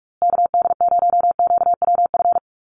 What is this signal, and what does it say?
Morse code, heard as a keyed tone, about 44 wpm, 700 Hz. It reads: KD0QWF
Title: USA Call Signs Repeated 3x 45wpm